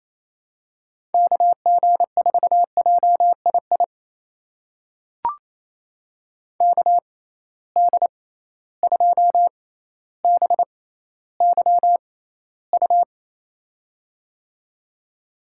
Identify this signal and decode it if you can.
Morse code, heard as a keyed tone, about 28 wpm, spaced as for 14 wpm, 700 Hz. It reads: KG4JII KD2BYU
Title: USA Call Signs - 6 Characters - 6x Character Spacing - 28wpm